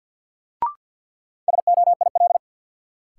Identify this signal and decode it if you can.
Morse code, heard as a keyed tone, about 52 wpm, 700 Hz. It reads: SOIL
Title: Sentences from Top 1500 Words - New Word Drill Alt - 50wpm